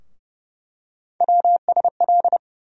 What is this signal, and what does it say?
Morse code, heard as a keyed tone, about 30 wpm, 700 Hz. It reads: WSL
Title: USA Call Sign Postfixes - 3 Letters 30wpm